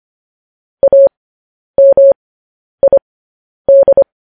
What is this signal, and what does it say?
Morse code, heard as a keyed tone, about 25 wpm, spaced as for 10 wpm, 550 Hz. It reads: AMID
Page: News Headlines in Morse Code at 10 WPM